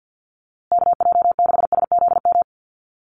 Morse code, heard as a keyed tone, about 48 wpm, 700 Hz. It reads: KJ6SZM
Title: USA Call Signs Repeated 2x 50wpm